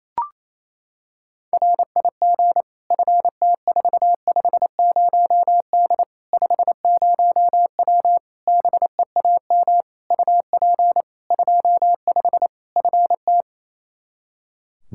Morse code, heard as a keyed tone, about 28 wpm, 700 Hz. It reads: RIG FT450D 50W BEAM UP 25 FT
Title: Practice QSO - Exchange 2 - Info - 28wpm